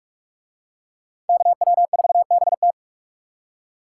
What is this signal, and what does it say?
Morse code, heard as a keyed tone, about 40 wpm, 700 Hz. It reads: KW4BT